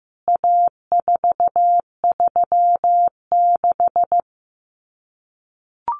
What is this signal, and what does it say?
Morse code, heard as a keyed tone, about 15 wpm, 700 Hz. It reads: A436
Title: CWT Exchange 15wpm